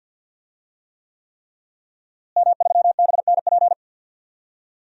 Morse code, heard as a keyed tone, about 51 wpm, 700 Hz. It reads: M3BNP
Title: DXCC-100 Call Signs - 5 Characters - Repeated 3x - 50wpm